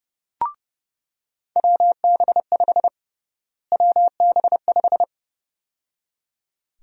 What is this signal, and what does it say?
Morse code, heard as a keyed tone, about 30 wpm, 700 Hz. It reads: WB5 WB5
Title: USA Call Sign Prefixes - Repeated 2x - 30wpm